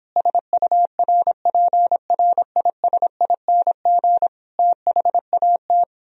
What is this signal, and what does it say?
Morse code, heard as a keyed tone, about 26 wpm, 700 Hz. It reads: SURPRISING THAT